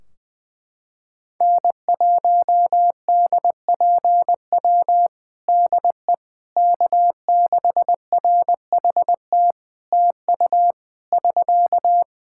Morse code, heard as a keyed tone, about 21 wpm, 700 Hz. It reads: N1DPW DE K6RHT TU <SK>
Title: Practice QSO - Exchange 4 - Ending - 20wpm